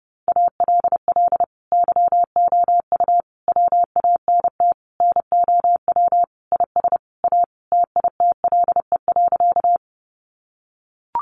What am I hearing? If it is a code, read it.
Morse code, heard as a keyed tone, about 28 wpm, 700 Hz. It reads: ALL YOU WANT NOW IS A TITLE.
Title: Sentences from Top 1200 Words 30wpm